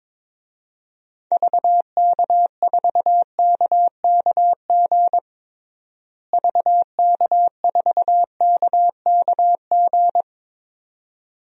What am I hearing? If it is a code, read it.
Morse code, heard as a keyed tone, about 22 wpm, 700 Hz. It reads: VK4KKG VK4KKG
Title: DXCC-100 Call Signs - 6 Characters - Repeated 2x - 22wpm